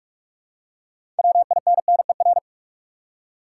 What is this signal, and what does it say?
Morse code, heard as a keyed tone, about 46 wpm, 700 Hz. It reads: WINNER
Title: Six Letter Words - 45wpm